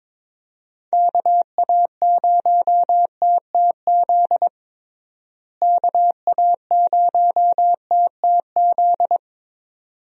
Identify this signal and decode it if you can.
Morse code, heard as a keyed tone, about 22 wpm, 700 Hz. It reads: KA0TTZ KA0TTZ
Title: USA Call Signs - 6 Characters - Repeated 2x - 22wpm